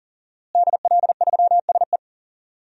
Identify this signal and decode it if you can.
Morse code, heard as a keyed tone, about 40 wpm, 700 Hz. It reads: DL3SE
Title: DXCC-100 Call Signs - 5 Characters - Repeated 2x - 40wpm